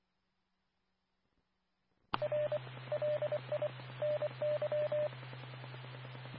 Morse code, heard as a keyed tone, about 24 wpm, 600 Hz. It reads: R LI NY
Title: Nassau and Suffolk County Amateur Repeaters